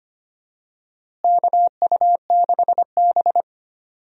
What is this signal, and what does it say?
Morse code, heard as a keyed tone, about 25 wpm, 700 Hz. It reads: KU6B
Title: USA Call Signs - 4 Characters - Repeated 3x - 25wpm